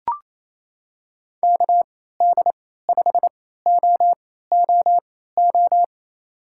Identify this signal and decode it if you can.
Morse code, heard as a keyed tone, about 28 wpm, spaced as for 16 wpm, 700 Hz. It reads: KD5OOO
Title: USA Call Signs - 6 Characters - 3x Character Spacing - 28wpm